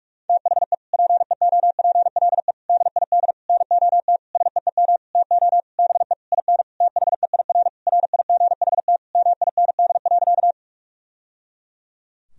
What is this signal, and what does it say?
Morse code, heard as a keyed tone, about 42 wpm, 700 Hz. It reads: THE PEOPLE DID NOT SEEM TO BE IN THEIR RIGHT MIND.
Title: Sentences from Top 1000 Words - Review Part 1 - 45wpm